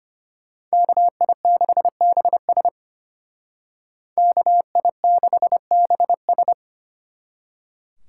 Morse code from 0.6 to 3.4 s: KI6BS

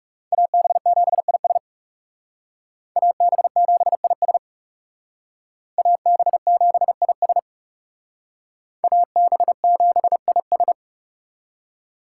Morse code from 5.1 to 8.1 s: AB7IS

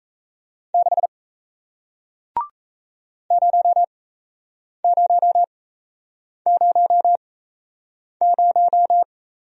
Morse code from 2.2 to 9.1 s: 0000

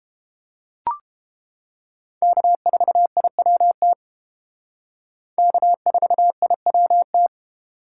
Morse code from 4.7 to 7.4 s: K4IWT

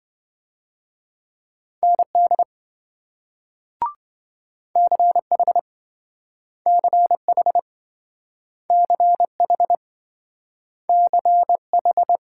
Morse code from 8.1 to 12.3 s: CH CH

